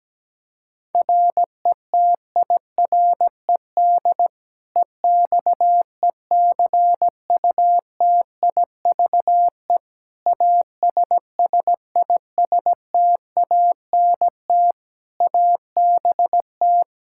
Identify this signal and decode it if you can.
Morse code, heard as a keyed tone, about 17 wpm, 700 Hz. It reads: RETIRED EXECUTIVE ASSISTANT ABT
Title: Practice QSO - Exchange 3 - More Info - 17wpm